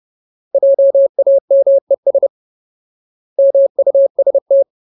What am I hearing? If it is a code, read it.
Morse code, heard as a keyed tone, about 30 wpm, 550 Hz. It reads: JAMES MUST
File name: Ultimate Word List / 30WPM Ultimate